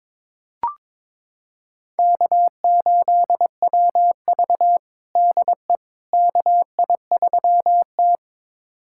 Morse code, heard as a keyed tone, about 22 wpm, 700 Hz. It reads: K8WV DE KI3T
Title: USA Callsign DE Callsign - 4 Characters - 22wpm